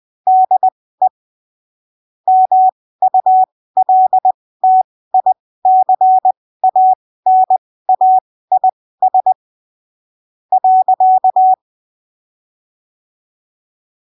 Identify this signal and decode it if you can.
Morse code, heard as a keyed tone, about 20 wpm, spaced as for 15 wpm, 750 Hz. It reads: DE MULTICANAIS .
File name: arquivos / texto / texto 13 20ppm